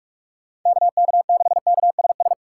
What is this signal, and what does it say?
Morse code, heard as a keyed tone, about 44 wpm, 700 Hz. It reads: KK6KSS
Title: USA Call Signs - 6 Characters - 4x Character Spacing - 45wpm